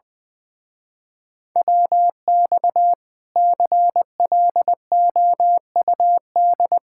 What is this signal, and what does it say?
Morse code, heard as a keyed tone, about 20 wpm, 700 Hz. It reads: WX CLOUD
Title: Speed-Racing Common QSO Phrases Encore - 20wpm